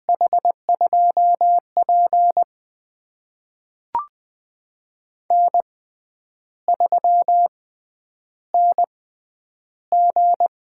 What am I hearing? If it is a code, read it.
Morse code, heard as a keyed tone, about 20 wpm, spaced as for 9 wpm, 700 Hz. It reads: H2P N3NG
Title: USA Call Signs - 4 Characters - 6x Character Spacing - 20wpm